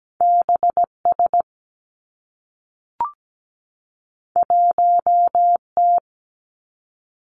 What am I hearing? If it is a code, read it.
Morse code, heard as a keyed tone, about 17 wpm, 700 Hz. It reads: B S 1 T
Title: Two Challenging Letter-Number Combinations 17wpm